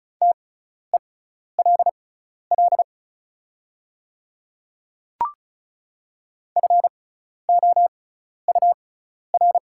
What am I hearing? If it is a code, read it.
Morse code, heard as a keyed tone, about 36 wpm, spaced as for 12 wpm, 700 Hz. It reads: TELL FOUR
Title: 6x Character Spacing - Top 300 Words - 35wpm